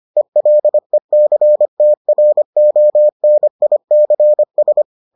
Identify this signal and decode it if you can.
Morse code, heard as a keyed tone, about 25 wpm, 600 Hz. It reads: ELECTRONICS